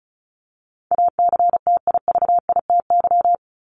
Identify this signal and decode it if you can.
Morse code, heard as a keyed tone, about 34 wpm, 700 Hz. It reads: ACTIVITY
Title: ES:S Suffix - Top 100 Words with Base Words 35wpm